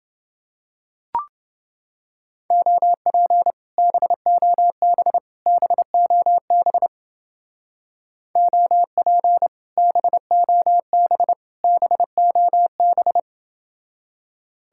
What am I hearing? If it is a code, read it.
Morse code, heard as a keyed tone, about 31 wpm, 700 Hz. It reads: OP BOB BOB OP BOB BOB